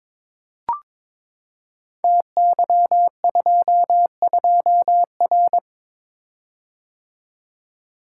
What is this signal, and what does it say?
Morse code, heard as a keyed tone, about 22 wpm, 700 Hz. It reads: TY22R
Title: Code Groups - 5 Letter:Numbers - 22wpm